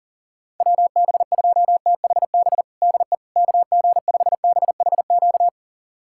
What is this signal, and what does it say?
Morse code, heard as a keyed tone, about 40 wpm, 700 Hz. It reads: WB2THB DE KG5BHQ
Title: USA Callsign DE Callsign 40wpm